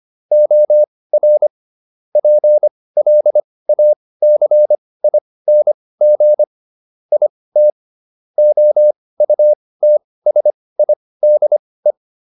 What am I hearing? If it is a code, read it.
Morse code, heard as a keyed tone, about 25 wpm, spaced as for 18 wpm, 600 Hz. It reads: OR PLACING IT OUTSIDE